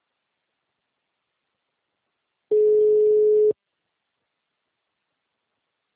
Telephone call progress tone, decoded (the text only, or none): ringback tone